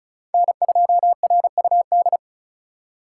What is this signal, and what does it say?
Morse code, heard as a keyed tone, about 36 wpm, 700 Hz. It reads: N2RUD
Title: USA Call Signs - 5 Characters - 2x Character Spacing - 35wpm